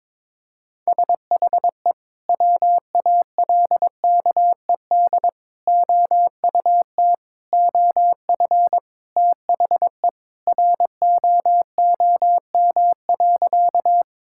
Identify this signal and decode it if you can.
Morse code, heard as a keyed tone, about 22 wpm, 700 Hz. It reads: SHE WALKED OUT OF THE ROOM.
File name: practice / 22wpm / Sentences with 3-Word Phrases - 22wpm